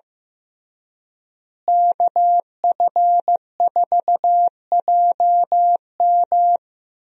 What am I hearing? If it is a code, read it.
Morse code, heard as a keyed tone, about 15 wpm, 700 Hz. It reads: KF4JM
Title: USA Call Signs - 5 Characters - Repeated 3x - 15wpm